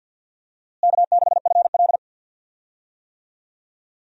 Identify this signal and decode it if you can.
Morse code, heard as a keyed tone, about 48 wpm, 700 Hz. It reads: K6FL
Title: USA Call Signs - 4 Characters - Repeated 3x - 50wpm